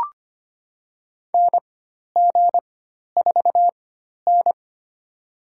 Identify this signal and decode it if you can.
Morse code, heard as a keyed tone, about 25 wpm, spaced as for 12 wpm, 700 Hz. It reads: NG4N
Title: USA Call Signs - 4 Characters - 4x Character Spacing - 25wpm